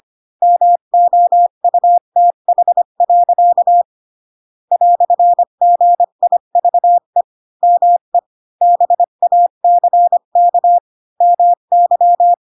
Morse code, heard as a keyed tone, about 25 wpm, 700 Hz. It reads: MOUTH. "GIVE ME BACK MY